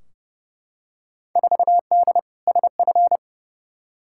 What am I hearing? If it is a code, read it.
Morse code, heard as a keyed tone, about 30 wpm, 700 Hz. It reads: 4D SF
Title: Field Day Exchanges Encore - 30wpm